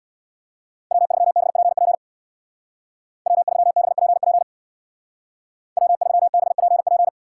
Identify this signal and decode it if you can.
Morse code, heard as a keyed tone, about 62 wpm, 700 Hz. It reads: W3BPP W3BPP W3BPP
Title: Speed Racing - USA Call Signs 50wpm